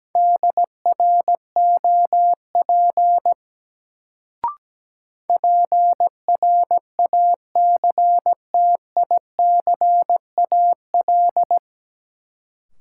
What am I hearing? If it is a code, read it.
Morse code, heard as a keyed tone, about 17 wpm, 700 Hz. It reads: DROP PRACTICAL
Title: Sentences from Top 1400 Words - New Word Drill Alt - 17wpm